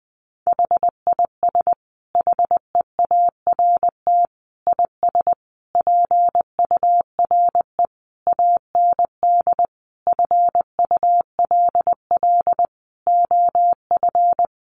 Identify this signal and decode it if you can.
Morse code, heard as a keyed tone, about 20 wpm, 700 Hz. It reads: HIS HEART IS PURE AND FULL OF
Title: Sentences from Top 1500 Words 20wpm